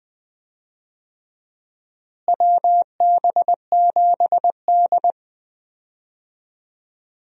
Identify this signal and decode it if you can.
Morse code, heard as a keyed tone, about 20 wpm, 700 Hz. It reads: WB7D
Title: USA Call Signs - 4 Characters - Repeated 3x - 20wpm